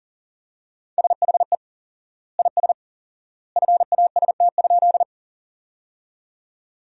Morse code, heard as a keyed tone, about 40 wpm, 700 Hz. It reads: SHE IS FAST?
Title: 4x Word-Spacing - Sentences from Top 400 Words - 40wpm